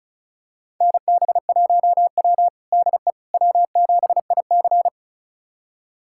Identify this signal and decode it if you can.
Morse code, heard as a keyed tone, about 34 wpm, 700 Hz. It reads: NB1W DE W7IC
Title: USA Callsign DE Callsign - 4 Characters - 35wpm